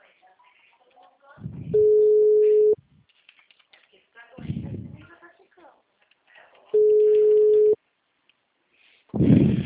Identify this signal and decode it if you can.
Ringback tone (Europe, 425 Hz), with speech.
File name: out- -2006- - - .78